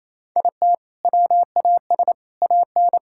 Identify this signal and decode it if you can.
Morse code, heard as a keyed tone, about 27 wpm, 700 Hz. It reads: IT WAS AN
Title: Sentences from Top 1000 Words - Review Part 6 - 28wpm